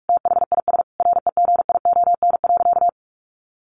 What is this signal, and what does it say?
Morse code, heard as a keyed tone, about 44 wpm, 700 Hz. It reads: THIS REGION.
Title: Sentences from Top 1500 Words 45wpm